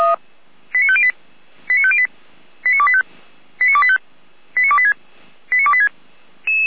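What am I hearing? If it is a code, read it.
Five-tone selective call (ZVEI), heard as quick runs of five tones, tones of about 70 ms. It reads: 89508 89508 89285 89285 89286 89286
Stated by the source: ZVEI five-tone selective call received on an SDR.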